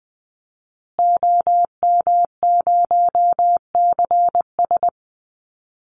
Morse code, heard as a keyed tone, about 20 wpm, 700 Hz. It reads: OM0CS
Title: Call Signs Repeated 2x - 20wpm